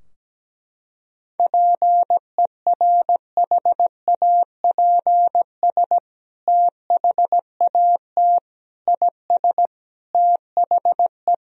Morse code, heard as a keyed tone, about 17 wpm, 700 Hz. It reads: PERHAPS THAT IS THE
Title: Sentences from Top 1000 Words - Review Part 4 - 17wpm